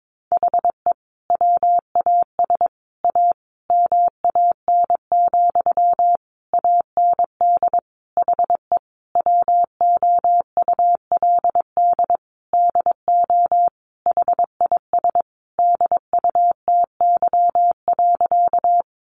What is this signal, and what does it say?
Morse code, heard as a keyed tone, about 22 wpm, 700 Hz. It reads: HE WAS A MAN, AND HE WOULD DO HIS DUTY.
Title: Sentences from Top 1500 Words 22wpm